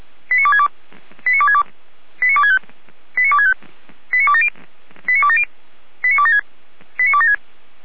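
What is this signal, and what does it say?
Five-tone selective call (ZVEI), heard as quick runs of five tones, tones of about 70 ms. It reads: 89262 89262 89265 89265 89270 89270 89276 89276
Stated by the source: ZVEI five-tone call 89280, received on SDR.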